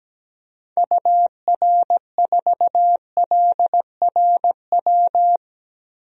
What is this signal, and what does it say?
Morse code, heard as a keyed tone, about 17 wpm, 700 Hz. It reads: UR4LRW
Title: DXCC-100 Call Signs - 6 Characters - Repeated 3x - 17wpm